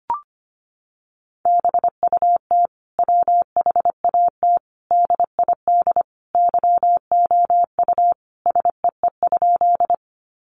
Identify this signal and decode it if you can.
Morse code, heard as a keyed tone, about 25 wpm, 700 Hz. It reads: BUT WHAT DID YOU SEE?
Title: Sentences from Top 100 Words 25wpm